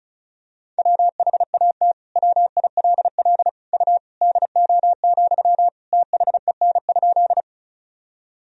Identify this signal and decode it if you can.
Morse code, heard as a keyed tone, about 34 wpm, 700 Hz. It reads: WHAT WILL U DO, THEN?
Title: Sentences from Top 100 Words and Abbreviations 35wpm